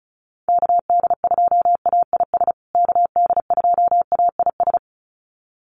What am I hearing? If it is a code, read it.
Morse code, heard as a keyed tone, about 34 wpm, 700 Hz. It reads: KD2AIS KD2AIS
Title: USA Call Signs Repeated 2x 35wpm